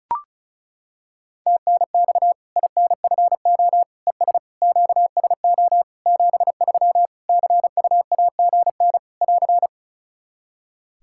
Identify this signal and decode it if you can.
Morse code, heard as a keyed tone, about 36 wpm, 700 Hz. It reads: TNX INFO ES QSO 73 CUAGN <AR>
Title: Practice QSO - Exchange 4 - Ending - 35wpm